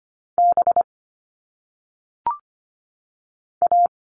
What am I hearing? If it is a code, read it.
Morse code, heard as a keyed tone, about 25 wpm, spaced as for 3 wpm, 700 Hz. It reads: BA
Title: All Cut Numbers - 1-Digit 25wpm